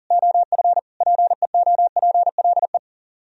Morse code, heard as a keyed tone, about 46 wpm, 700 Hz. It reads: OF PEOPLE
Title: Speed-Racing - Sets of 3 Words 40wpm